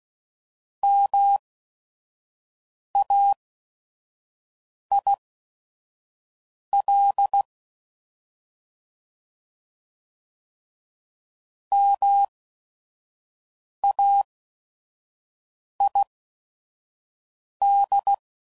Morse code, heard as a keyed tone, about 16 wpm, spaced as for 5 wpm, 800 Hz. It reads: MAIL MAID